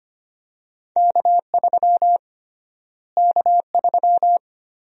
Morse code, heard as a keyed tone, about 25 wpm, 700 Hz. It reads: K3 K3
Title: USA Call Sign Prefixes - Repeated 2x - 25wpm